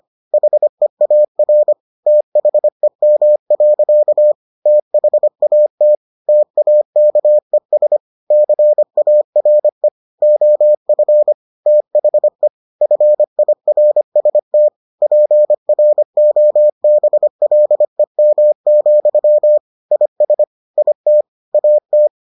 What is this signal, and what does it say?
Morse code, heard as a keyed tone, about 25 wpm, 600 Hz. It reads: HEAR THEM. THAT TAKES CARE OF THE FIRST PROBLEM, IS IT AT